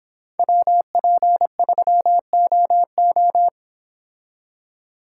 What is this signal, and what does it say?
Morse code, heard as a keyed tone, about 26 wpm, 700 Hz. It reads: WP3OO